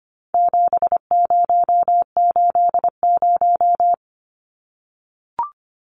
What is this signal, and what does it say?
Morse code, heard as a keyed tone, about 25 wpm, 700 Hz. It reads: 7080
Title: Numbers - 4-Digit 25wpm